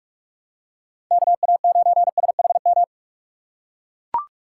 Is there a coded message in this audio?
Yes: Morse code from 1.1 to 2.8 s, heard as a keyed tone, about 41 wpm, 700 Hz. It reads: KA9SHM